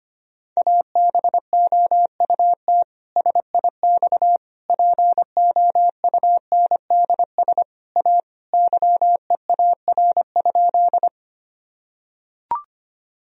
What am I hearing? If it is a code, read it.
Morse code, heard as a keyed tone, about 25 wpm, 700 Hz. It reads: ABOUT SIX POUNDS A YEAR?
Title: Sentences from Top 1000 Words - Review Part 2 - 25wpm